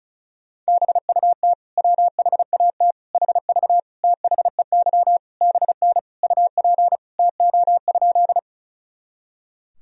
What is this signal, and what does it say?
Morse code, heard as a keyed tone, about 34 wpm, 700 Hz. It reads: BUT WHAT HV THEY BN UP TO?
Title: Sentences from Top 100 Words and Abbreviations 35wpm